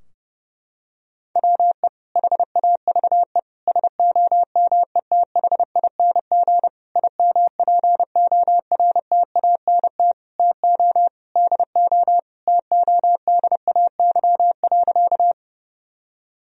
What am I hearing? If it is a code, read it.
Morse code, heard as a keyed tone, about 30 wpm, 700 Hz. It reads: WE HAVE SOMETHING IMPORTANT TO DO TODAY.
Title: Sentences from Top 1000 Words - Review Part 4 - 30wpm